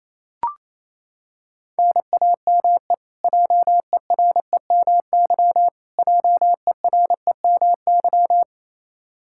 Morse code, heard as a keyed tone, about 28 wpm, 700 Hz. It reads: NAME JEREMY JEREMY
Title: Practice QSO - Exchange 1 - Beginning - 28wpm